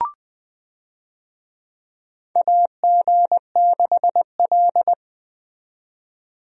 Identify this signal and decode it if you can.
Morse code, heard as a keyed tone, about 20 wpm, 700 Hz. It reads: AG6L
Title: USA Call Signs - 4 Characters - 20wpm